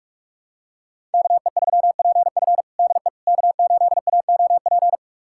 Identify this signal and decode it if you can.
Morse code, heard as a keyed tone, about 42 wpm, 700 Hz. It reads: KE3PF DE K8AOP
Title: USA Callsign DE Callsign - 5 Characters - 45wpm